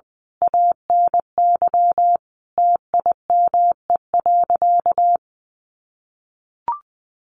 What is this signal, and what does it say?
Morse code, heard as a keyed tone, about 20 wpm, 700 Hz. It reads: ANY TIME.
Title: Sentences from Top 1500 Words 20wpm